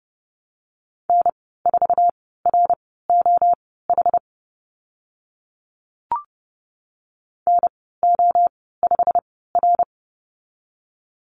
Morse code, heard as a keyed tone, about 30 wpm, spaced as for 17 wpm, 700 Hz. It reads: N4ROH NO5R
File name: Lesson 15g - Number 5 - Call Signs 30wpm